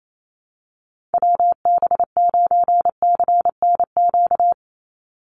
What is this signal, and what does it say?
Morse code, heard as a keyed tone, about 28 wpm, 700 Hz. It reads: WB9CNQ
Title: USA Call Signs 28wpm